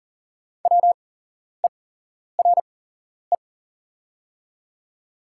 Morse code, heard as a keyed tone, about 39 wpm, spaced as for 11 wpm, 700 Hz. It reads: WERE
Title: 8x Character Spacing - Top 100 Words - 40wpm